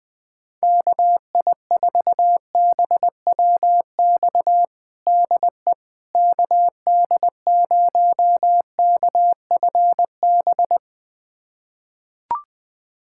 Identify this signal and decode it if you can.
Morse code, heard as a keyed tone, about 20 wpm, 700 Hz. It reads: KI4BWX DE KD0KFB
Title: USA Callsign DE Callsign - 6 Characters - 20wpm